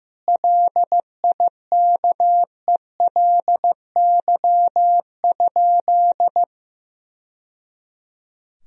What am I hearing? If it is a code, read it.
Morse code, heard as a keyed tone, about 16 wpm, 700 Hz. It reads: LIKELY?
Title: Sentences from Top 1000 Words - Review Part 4 - 15wpm